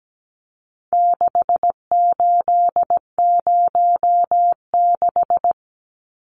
Morse code, heard as a keyed tone, about 17 wpm, 700 Hz. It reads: 6806
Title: Numbers - 4-Digit 17wpm